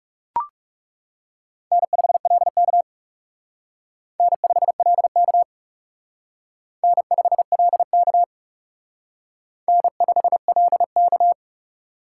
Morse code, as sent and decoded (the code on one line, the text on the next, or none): -. ..... .-.. -.- / -. ..... .-.. -.- / -. ..... .-.. -.- / -. ..... .-.. -.-
N5LK N5LK N5LK N5LK